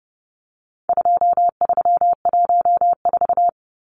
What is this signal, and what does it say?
Morse code, heard as a keyed tone, about 30 wpm, 700 Hz. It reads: 2314
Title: Numbers - 4-Digit 30wpm